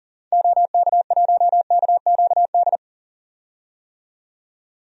Morse code, heard as a keyed tone, about 40 wpm, 700 Hz. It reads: OK1KQD
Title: DXCC-100 Call Signs - 6 Characters - 40wpm